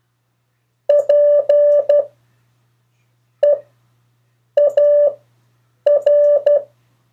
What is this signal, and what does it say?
Morse code, heard as a keyed tone, about 12 wpm, spaced as for 6 wpm, 580 Hz. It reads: PEAR
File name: 3of4-words-2